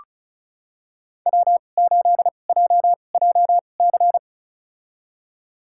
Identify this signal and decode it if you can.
Morse code, heard as a keyed tone, about 34 wpm, spaced as for 25 wpm, 690 Hz. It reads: W8JJC
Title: USA Call Signs - 5 Characters - 2x Character Spacing - 35wpm